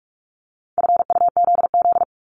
Morse code, heard as a keyed tone, about 46 wpm, 700 Hz. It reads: FUZZ
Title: Multi-letter Phonemes Repeated 3x 45wpm